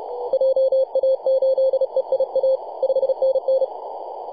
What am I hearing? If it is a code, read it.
Morse code, heard as a keyed tone, about 31 wpm, 530 Hz. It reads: JA8EIA 5NN
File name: posts / 2015-oct-03-2319z-tx3x-21-023mhz-cw